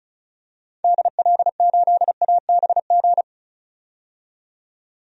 Morse code, heard as a keyed tone, about 34 wpm, 700 Hz. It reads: DL8ABG
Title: DXCC-100 Call Signs - 6 Characters - 35wpm